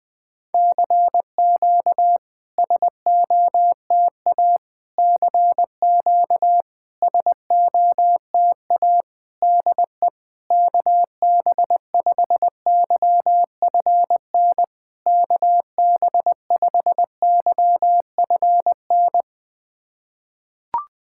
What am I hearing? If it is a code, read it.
Morse code, heard as a keyed tone, about 20 wpm, 700 Hz. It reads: CQ SOTA CQ SOTA DE KB5YFN KB5YFN
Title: SOTA QSO Practice 20wpm